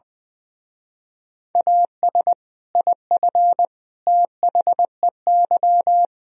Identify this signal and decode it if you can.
Morse code, heard as a keyed tone, about 20 wpm, 700 Hz. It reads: AS IF THEY
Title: Speed-Racing - Sets of 4 Words 20wpm